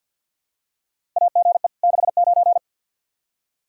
Morse code, heard as a keyed tone, about 52 wpm, 700 Hz. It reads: AGE 69
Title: Practice QSO - Exchange 3 - More Info - 50wpm